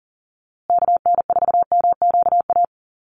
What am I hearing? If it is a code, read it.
Morse code, heard as a keyed tone, about 40 wpm, 700 Hz. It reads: KN4MQA
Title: USA Call Signs Repeated 3x 40wpm